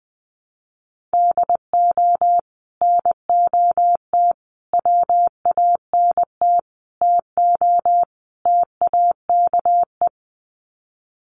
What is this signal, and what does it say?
Morse code, heard as a keyed tone, about 20 wpm, 700 Hz. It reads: DO NOT WANT TO TAKE
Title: Sets of 5 Words Encore2 20wpm